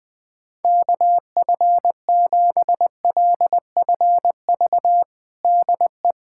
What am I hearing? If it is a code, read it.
Morse code, heard as a keyed tone, about 20 wpm, 700 Hz. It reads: KF7LFV DE
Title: USA Callsign DE Callsign 20wpm